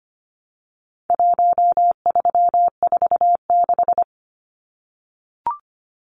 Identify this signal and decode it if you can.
Morse code, heard as a keyed tone, about 25 wpm, 700 Hz. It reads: 1346
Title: Numbers - Contest Style - 4-Digit 25wpm